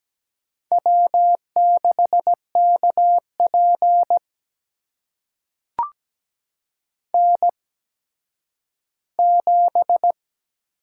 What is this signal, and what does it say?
Morse code, heard as a keyed tone, about 17 wpm, spaced as for 10 wpm, 700 Hz. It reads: W6KP N7
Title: USA Call Signs - 4 Characters - 8x Character Spacing - 17wpm